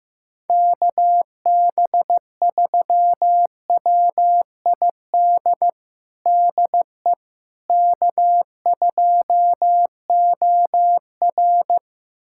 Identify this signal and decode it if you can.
Morse code, heard as a keyed tone, about 15 wpm, 700 Hz. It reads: KB3WID DE K2OR
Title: USA Callsign DE Callsign 15wpm